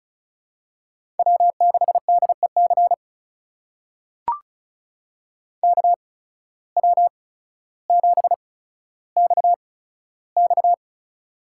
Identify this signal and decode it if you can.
Morse code, heard as a keyed tone, about 33 wpm, spaced as for 15 wpm, 700 Hz. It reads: W6DEC KW7XX
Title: USA Call Signs - 5 Characters - 8x Character Spacing - 35wpm